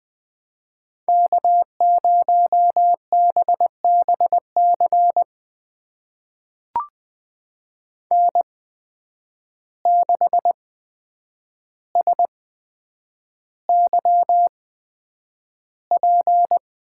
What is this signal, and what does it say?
Morse code, heard as a keyed tone, about 20 wpm, spaced as for 9 wpm, 700 Hz. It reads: K0BBC N6SYP
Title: USA Call Signs - 5 Characters - 8x Character Spacing - 20wpm